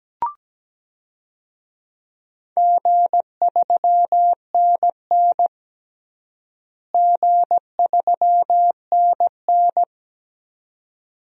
Morse code, heard as a keyed tone, about 17 wpm, 700 Hz. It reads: G3NN G3NN
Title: DXCC-100 Call Signs - 4 Characters - Repeated 2x - 17wpm